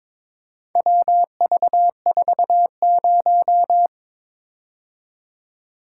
Morse code, heard as a keyed tone, about 22 wpm, 700 Hz. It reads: WV40
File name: practice / 22wpm / Four Letter-Number Combinations - 22wpm